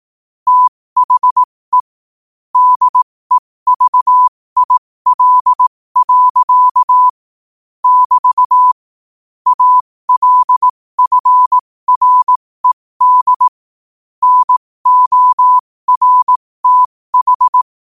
Morse code, heard as a keyed tone, about 18 wpm, spaced as for 15 wpm, 1000 Hz. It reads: THE DEVIL. <BT> ALFRED NORTH
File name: qotdfiles / 15wpm